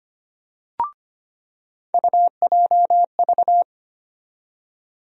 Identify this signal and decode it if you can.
Morse code, heard as a keyed tone, about 25 wpm, 700 Hz. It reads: UJV
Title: USA Call Sign Postfixes - 3 Letters 25wpm